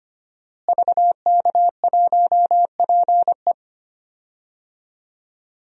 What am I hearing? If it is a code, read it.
Morse code, heard as a keyed tone, about 25 wpm, 700 Hz. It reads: VK1PE
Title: DXCC-100 Call Signs - 5 Characters - Repeated 3x - 25wpm